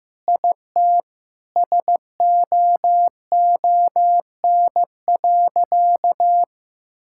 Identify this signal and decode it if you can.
Morse code, heard as a keyed tone, about 15 wpm, 700 Hz. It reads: IT SOON.
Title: Sentences from Top 1000 Words - Review Part 5 - 15wpm